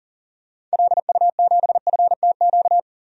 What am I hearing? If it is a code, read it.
Morse code, heard as a keyed tone, about 40 wpm, 700 Hz. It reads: LU7FTQ
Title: DXCC-100 Call Signs - 6 Characters - 40wpm